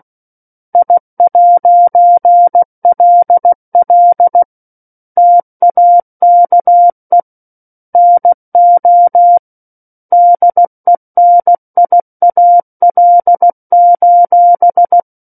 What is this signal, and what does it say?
Morse code, heard as a keyed tone, about 16 wpm, 700 Hz. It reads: I'LL TAKE NO DENIAL:
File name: 03-Chapter-03 16 WPM